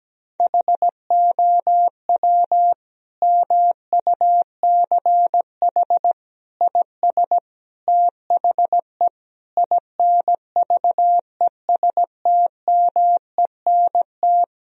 Morse code, heard as a keyed tone, about 17 wpm, 700 Hz. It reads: HOW MUCH IS THE INVESTMENT
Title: Sentences from Top 1000 Words - Review Part 4 - 17wpm